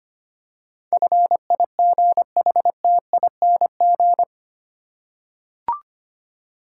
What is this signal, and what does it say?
Morse code, heard as a keyed tone, about 25 wpm, 700 Hz. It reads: FIGHTING
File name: 8 Letter Words - First 3 Letters Given - 25wpm